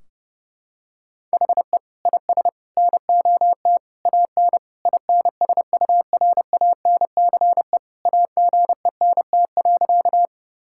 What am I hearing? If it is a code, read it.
Morse code, heard as a keyed tone, about 30 wpm, 700 Hz. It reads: HE IS NOT AN INSURANCE AGENT.
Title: Sentences from Top 1600 Words 30wpm